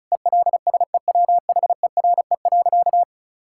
Morse code, heard as a keyed tone, about 36 wpm, 700 Hz. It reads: ELSEWHERE.
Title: Sentences from Top 1600 Words 35wpm